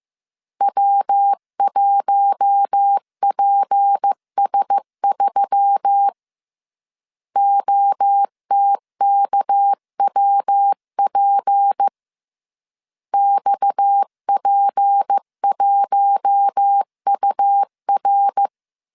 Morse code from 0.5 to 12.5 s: W1PS3 OTKWP